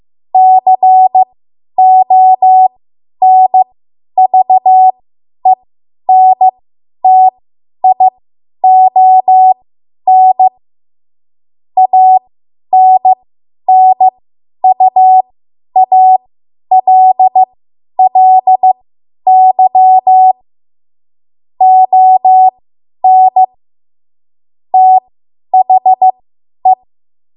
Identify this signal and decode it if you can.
Morse code, heard as a keyed tone, about 15 wpm, spaced as for 10 wpm, 750 Hz. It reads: CONVENTION ANNUALLY ON THE
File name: Morse / 10WPM